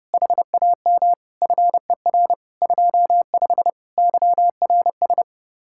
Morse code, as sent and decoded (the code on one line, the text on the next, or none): .... .- -- / ..-. .-. / ..--- ..... / -.-- .-. ...
HAM FER 25 YRS